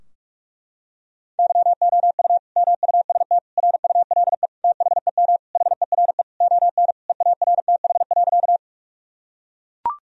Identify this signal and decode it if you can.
Morse code, heard as a keyed tone, about 41 wpm, 700 Hz. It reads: YOU MUST RULE THEM HERE ON EARTH.
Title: Sentences from Top 1000 Words - Review Part 2 - 45wpm